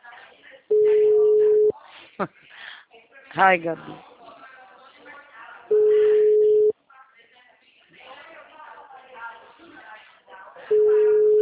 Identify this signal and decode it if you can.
Ringback tone (Europe, 425 Hz), with speech.
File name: out- -2010- - - .48735